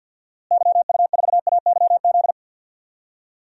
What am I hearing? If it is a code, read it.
Morse code, heard as a keyed tone, about 50 wpm, 700 Hz. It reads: YU4AYZ
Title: DXCC-100 Call Signs - 6 Characters - Repeated 3x - 50wpm